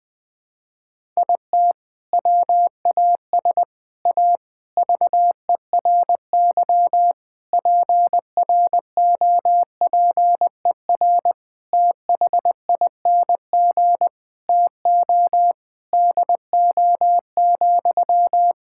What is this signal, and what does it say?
Morse code, heard as a keyed tone, about 20 wpm, 700 Hz. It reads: IT WAS A VERY PROPER THING TO DO,
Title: Sentences from Top 1000 Words - Review Part 5 - 20wpm